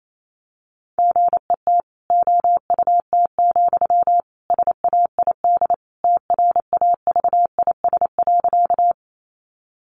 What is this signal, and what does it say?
Morse code, heard as a keyed tone, about 28 wpm, 700 Hz. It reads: GET OUT, SAID TRAVIS.
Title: A Sound Of Thunder 28wpm